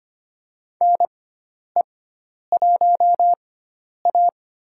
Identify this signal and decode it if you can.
Morse code, heard as a keyed tone, about 25 wpm, spaced as for 10 wpm, 700 Hz. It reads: NE1A
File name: Lesson 16g - Letter c - Call Signs 25wpm